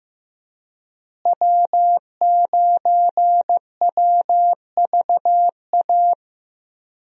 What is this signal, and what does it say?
Morse code, heard as a keyed tone, about 15 wpm, 700 Hz. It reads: W9WVA
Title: USA Call Signs - 5 Characters - Repeated 2x - 15wpm